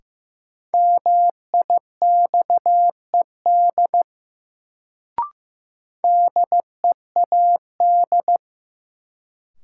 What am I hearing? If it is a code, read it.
Morse code, heard as a keyed tone, about 15 wpm, 700 Hz. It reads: MIXED DEAD
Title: Sentences from Top 1500 Words - New Word Drill - 15wpm